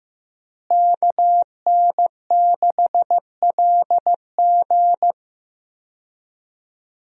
Morse code, heard as a keyed tone, about 15 wpm, 700 Hz. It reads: KN6LG
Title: USA Call Signs - 5 Characters - Repeated 2x - 15wpm